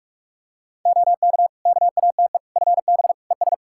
Morse code, heard as a keyed tone, about 46 wpm, 700 Hz. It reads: OK KATE FB ES
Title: Practice QSO - Exchange 2 - Info - 45wpm